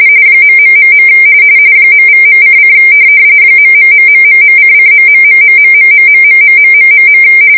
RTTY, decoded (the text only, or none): .... .=.:: ......::..............